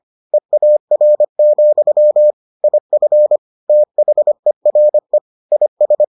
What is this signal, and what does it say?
Morse code, heard as a keyed tone, about 25 wpm, 600 Hz. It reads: EAR, IF THERE IS